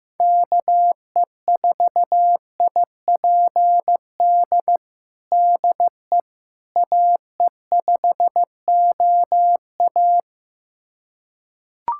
Morse code, heard as a keyed tone, about 15 wpm, 700 Hz. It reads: KE4IPD DE AE5OA
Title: USA Callsign DE Callsign 15wpm